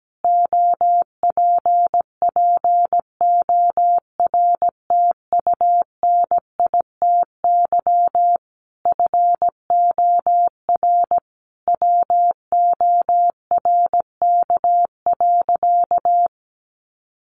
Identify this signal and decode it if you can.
Morse code, heard as a keyed tone, about 17 wpm, 700 Hz. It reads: OPPORTUNITY FOR WORK.
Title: Sentences from Top 1400 Words 17wpm